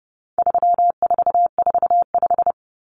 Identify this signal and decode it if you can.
Morse code, heard as a keyed tone, about 30 wpm, 700 Hz. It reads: 3445
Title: Numbers - 4-Digit 30wpm